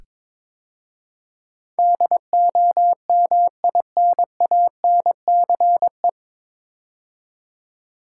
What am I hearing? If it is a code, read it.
Morse code, heard as a keyed tone, about 22 wpm, 700 Hz. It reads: DOMINANCE
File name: 9 Letter Words - First 3 Letters Given - 22wpm